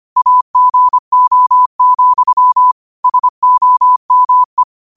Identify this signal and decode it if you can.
Morse code, heard as a keyed tone, about 25 wpm, 1000 Hz. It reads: AGO, SOME
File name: qotdfiles / 25wpm